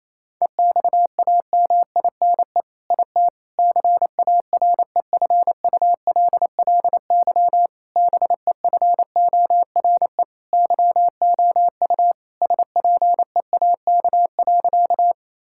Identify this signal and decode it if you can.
Morse code, heard as a keyed tone, about 28 wpm, 700 Hz. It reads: EXAMINE IT CAREFULLY BEFORE YOU SPEAK.
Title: Sentences from Top 1600 Words 28wpm